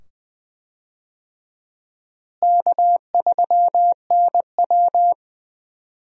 Morse code, heard as a keyed tone, about 20 wpm, 700 Hz. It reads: K3NW
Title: USA Call Signs - 4 Characters - Repeated 2x - 20wpm